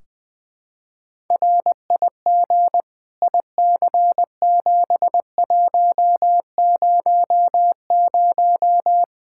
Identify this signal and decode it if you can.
Morse code, heard as a keyed tone, about 20 wpm, 700 Hz. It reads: RIG IC7100
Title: K1USN SST QSO Practice 20wpm